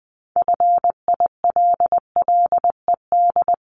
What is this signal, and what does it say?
Morse code, heard as a keyed tone, about 20 wpm, 700 Hz. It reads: FILLED
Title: Multi-letter Phonemes Repeated 3x 20wpm